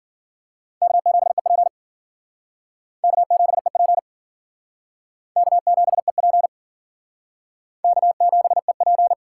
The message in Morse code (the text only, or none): K7EP K7EP K7EP K7EP